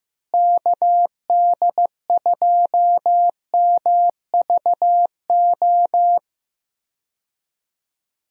Morse code, heard as a keyed tone, about 15 wpm, 700 Hz. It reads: KD2MVO
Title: USA Call Signs - 6 Characters - Repeated 2x - 15wpm